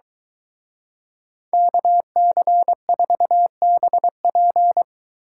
Morse code, heard as a keyed tone, about 23 wpm, 700 Hz. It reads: KC4BP